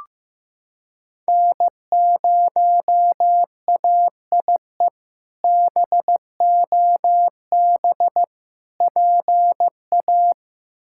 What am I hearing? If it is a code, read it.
Morse code, heard as a keyed tone, about 15 wpm, 700 Hz. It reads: N0AIE BOB PA
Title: K1USN SST QSO Practice - 15wpm